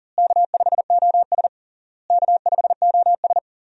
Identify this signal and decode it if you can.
Morse code, heard as a keyed tone, about 37 wpm, 700 Hz. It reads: K5OS K5OS
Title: USA Call Signs - 4 Characters - Repeated 3x - 40wpm